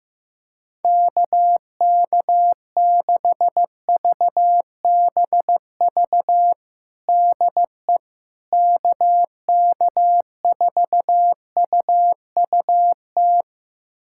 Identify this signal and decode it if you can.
Morse code, heard as a keyed tone, about 15 wpm, 700 Hz. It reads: KK6VBV DE KK4UUT
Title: USA Callsign DE Callsign - 6 Characters - 15wpm